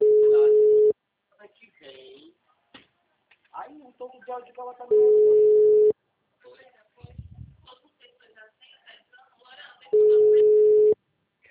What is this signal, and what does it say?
Ringback tone (Europe, 425 Hz), with speech.
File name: out- -2024- - - .67532